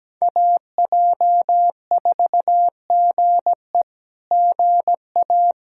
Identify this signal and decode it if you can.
Morse code, heard as a keyed tone, about 17 wpm, 700 Hz. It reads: AJ4GE GA
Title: SOTA QSO Practice 17wpm